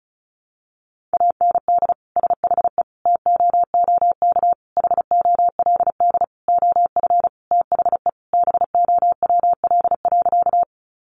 Morse code, heard as a keyed tone, about 34 wpm, 700 Hz. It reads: AND SHE TOOK HOLD OF THE BOWL.
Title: Sentences from Top 1200 Words 35wpm